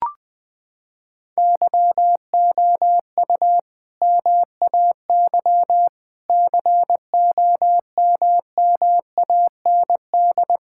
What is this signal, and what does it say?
Morse code, heard as a keyed tone, about 21 wpm, 700 Hz. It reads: YOU MAY COMMAND
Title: Sentences from Top 1600 Words 20wpm